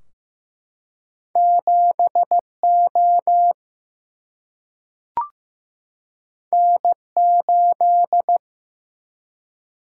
Morse code, heard as a keyed tone, about 15 wpm, 700 Hz. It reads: 7 O N 8
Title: Single-Letter Single-Number 15wpm